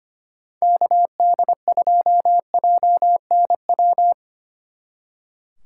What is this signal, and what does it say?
Morse code, heard as a keyed tone, about 32 wpm, 700 Hz. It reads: KD2JNW